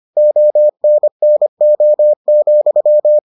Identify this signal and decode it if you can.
Morse code, heard as a keyed tone, about 25 wpm, 600 Hz. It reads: ONNO,